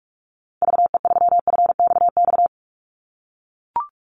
Morse code, heard as a keyed tone, about 44 wpm, 700 Hz. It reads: VE3FXX
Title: Call Signs - 45wpm